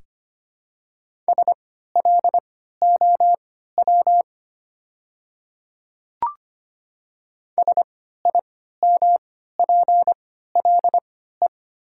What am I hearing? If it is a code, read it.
Morse code, heard as a keyed tone, about 25 wpm, spaced as for 14 wpm, 700 Hz. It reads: SLOW SIMPLE
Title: 3x Character Spacing - Top 500 Words - 25wpm